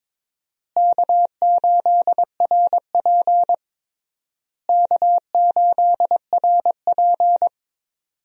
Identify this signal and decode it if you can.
Morse code, heard as a keyed tone, about 22 wpm, 700 Hz. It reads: K8RP K8RP
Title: USA Call Signs - 4 Characters - Repeated 3x - 22wpm